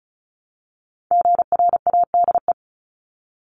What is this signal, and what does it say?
Morse code, heard as a keyed tone, about 34 wpm, 700 Hz. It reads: GRADE
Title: Top 1501-1750 Words 35wpm